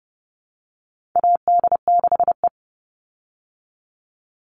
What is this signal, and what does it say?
Morse code, heard as a keyed tone, about 29 wpm, 700 Hz. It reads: AD6E
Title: Call Signs Repeated 3x - 30wpm